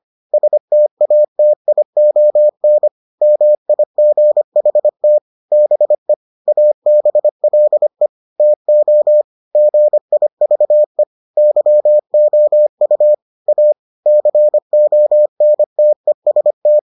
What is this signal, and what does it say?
Morse code, heard as a keyed tone, about 25 wpm, 600 Hz. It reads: STATION MIGHT BE ABLE TO GIVE YOU A CONTEST